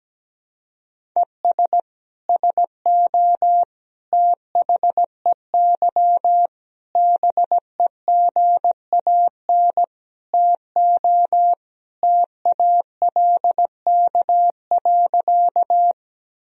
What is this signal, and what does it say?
Morse code, heard as a keyed tone, about 17 wpm, 700 Hz. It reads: ES SO THEY BEGAN TO TALK.
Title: Sentences from Top 400 Words and Abbreviations 17wpm